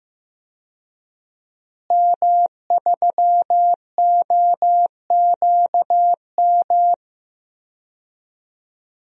Morse code, heard as a keyed tone, about 16 wpm, 700 Hz. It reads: M3OQM
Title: DXCC-100 Call Signs - 5 Characters - Repeated 3x - 15wpm